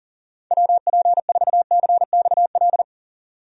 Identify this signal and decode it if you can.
Morse code, heard as a keyed tone, about 37 wpm, 700 Hz. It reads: WP4CXL
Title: USA Call Signs - 6 Characters - 40wpm